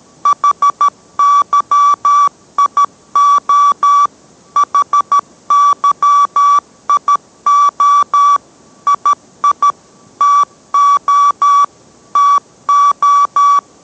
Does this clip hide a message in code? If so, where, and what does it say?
Yes: Morse code from 0.3 to 13.6 s, heard as a keyed tone, about 13 wpm, 1150 Hz. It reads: HYIO HYIO II TO TO